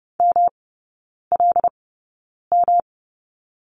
Morse code, heard as a keyed tone, about 30 wpm, spaced as for 9 wpm, 700 Hz. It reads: MLM
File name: Lesson 18e - Letter m - Three Characters 30wpm